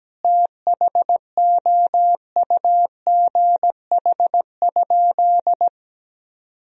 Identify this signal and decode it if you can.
Morse code, heard as a keyed tone, about 17 wpm, 700 Hz. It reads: THOUGH?
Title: Sentences from Top 1000 Words - Review Part 2 - 17wpm